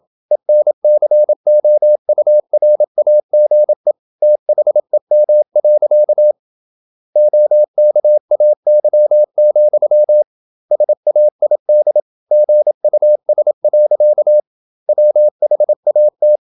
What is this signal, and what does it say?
Morse code, heard as a keyed tone, about 27 wpm, 600 Hz. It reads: ENCOURAGE THEM. OKAY, SAID GUS. WHAT
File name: Never 0027wpm